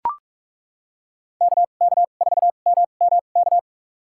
Morse code, heard as a keyed tone, about 41 wpm, spaced as for 31 wpm, 700 Hz. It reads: KK4MMK
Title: USA Call Signs - 2x Character Spacing - 45wpm